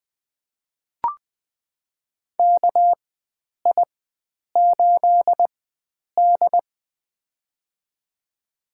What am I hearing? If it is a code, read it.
Morse code, heard as a keyed tone, about 20 wpm, spaced as for 9 wpm, 700 Hz. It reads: KI8D
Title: USA Call Signs - 4 Characters - 4x Character Spacing - 20wpm